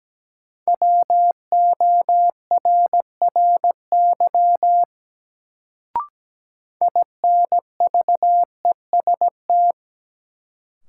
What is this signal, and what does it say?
Morse code, heard as a keyed tone, about 17 wpm, 700 Hz. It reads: WORRY INVEST
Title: Sentences from Top 1100 Words - New Word Drill - 17wpm